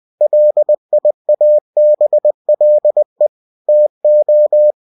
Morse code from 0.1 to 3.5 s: LIABLE